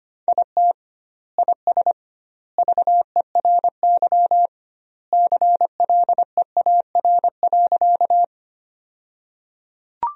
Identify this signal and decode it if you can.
Morse code, heard as a keyed tone, about 25 wpm, 700 Hz. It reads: IT IS VERY CLEAR.
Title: 2x Word-Spacing - Sentences from Top 500 Words - 25wpm